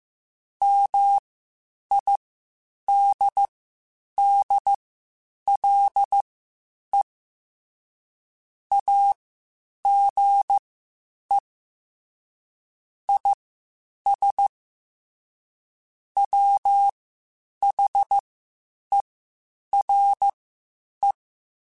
Morse code, heard as a keyed tone, about 15 wpm, spaced as for 8 wpm, 800 Hz. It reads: MIDDLE AGE IS WHERE
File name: Spaced / Chuckles Spaced-mp3s / MiddleAge sp @13